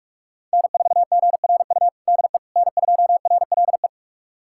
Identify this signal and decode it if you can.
Morse code, heard as a keyed tone, about 41 wpm, 700 Hz. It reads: N4GRU DE N2RLE
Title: USA Callsign DE Callsign - 5 Characters - 45wpm